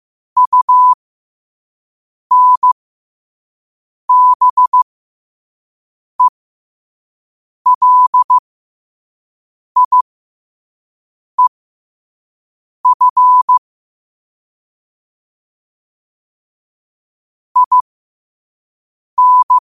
Morse code, heard as a keyed tone, about 15 wpm, spaced as for 5 wpm, 1000 Hz. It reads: UNBELIEF IN